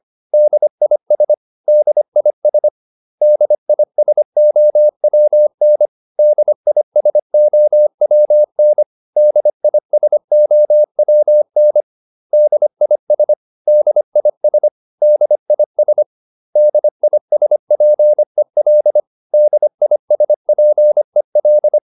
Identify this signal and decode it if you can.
Morse code, heard as a keyed tone, about 25 wpm, 600 Hz. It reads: DIS DIS DISOWN DISOWN DISOWN DIS DIS DIS DISPEL DISPEL